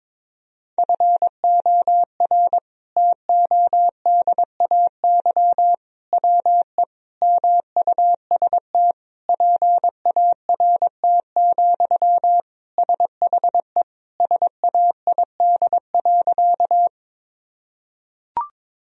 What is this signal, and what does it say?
Morse code, heard as a keyed tone, about 22 wpm, 700 Hz. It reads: FOR TODAY WE MUST PART, SHE SAID.
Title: Sentences from Top 1000 Words - Review Part 1 - 22wpm